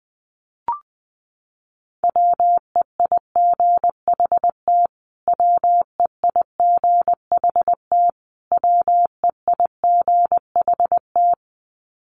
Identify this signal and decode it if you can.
Morse code, heard as a keyed tone, about 19 wpm, 700 Hz. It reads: WEIGHT WEIGHT WEIGHT
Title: Top 500 Words Repeated 20wpm